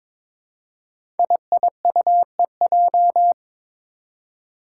Morse code, heard as a keyed tone, about 23 wpm, 700 Hz. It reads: IIUEJ
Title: Five Challenging Letter-Number Combinations 22wpm